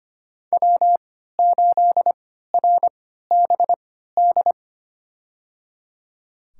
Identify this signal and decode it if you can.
Morse code, heard as a keyed tone, about 25 wpm, spaced as for 14 wpm, 700 Hz. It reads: W8RBD
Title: USA Call Signs - 5 Characters - 3x Character Spacing - 25wpm